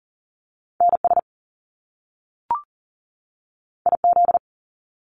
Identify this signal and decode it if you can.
Morse code, heard as a keyed tone, about 39 wpm, 700 Hz. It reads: NS IZ
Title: Top Two Letter Combinations in Words 40wpm